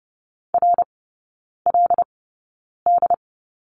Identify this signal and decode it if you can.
Morse code, heard as a keyed tone, about 30 wpm, spaced as for 9 wpm, 700 Hz. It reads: RLD
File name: 081 Lesson 14e - Number 2 - Three Characters 30wpm